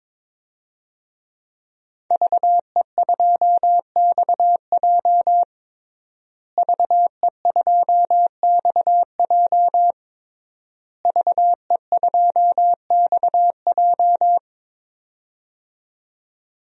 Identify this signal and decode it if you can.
Morse code, heard as a keyed tone, about 22 wpm, 700 Hz. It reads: VE2XJ VE2XJ VE2XJ
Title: DXCC-100 Call Signs - 5 Characters - Repeated 3x - 22wpm